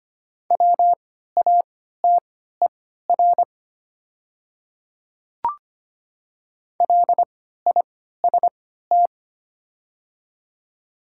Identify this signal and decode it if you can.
Morse code, heard as a keyed tone, about 25 wpm, spaced as for 14 wpm, 700 Hz. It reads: WATER LIST
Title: 3x Character Spacing - Top 500 Words - 25wpm